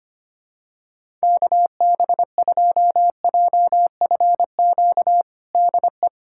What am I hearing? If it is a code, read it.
Morse code, heard as a keyed tone, about 25 wpm, 700 Hz. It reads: KB2JFQ DE
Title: USA Callsign DE Callsign - 6 Characters - 25wpm